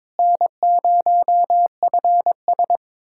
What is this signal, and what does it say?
Morse code, heard as a keyed tone, about 22 wpm, 700 Hz. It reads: N0FS
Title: USA Callsign DE Callsign - 4 Characters - 22wpm